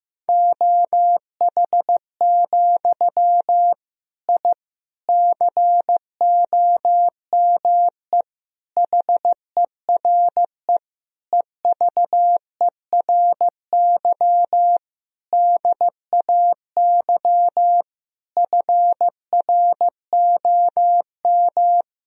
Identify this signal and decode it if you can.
Morse code, heard as a keyed tone, about 16 wpm, 700 Hz. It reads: OH, I COME HERE EVERY DAY FROM
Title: Sentences from Top 1000 Words - Review Part 2 - 15wpm